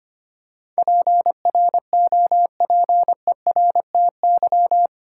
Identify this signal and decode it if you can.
Morse code, heard as a keyed tone, about 25 wpm, 700 Hz. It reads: PROPERTY
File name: 8 Letter Words - First 3 Letters Given - 25wpm